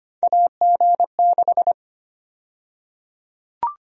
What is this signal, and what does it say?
Morse code, heard as a keyed tone, about 25 wpm, 700 Hz. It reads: AG6
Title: USA Call Sign Prefixes - 25wpm